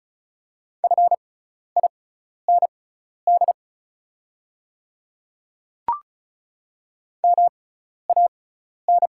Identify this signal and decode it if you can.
Morse code, heard as a keyed tone, about 36 wpm, spaced as for 12 wpm, 700 Hz. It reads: FIND MAN
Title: 6x Character Spacing - Top 200 Words - 35wpm